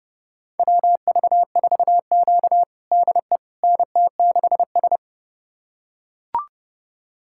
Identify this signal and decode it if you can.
Morse code, heard as a keyed tone, about 30 wpm, 700 Hz. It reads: WV4Q DE NT6S
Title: USA Callsign DE Callsign - 4 Characters - 30wpm